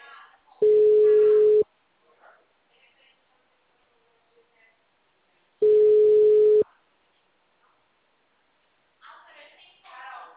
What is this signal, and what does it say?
Ringback tone (Europe, 425 Hz), with speech.
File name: out- -2006- - - .13687